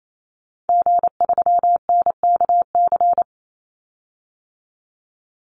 Morse code, heard as a keyed tone, about 28 wpm, 700 Hz. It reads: G3NKC
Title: Call Signs Repeated 3x - 28wpm